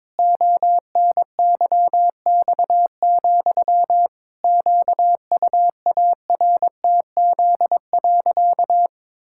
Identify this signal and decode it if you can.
Morse code, heard as a keyed tone, about 22 wpm, 700 Hz. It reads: ONYX, QUARTZ.
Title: Pangram Challenge v2 - 22wpm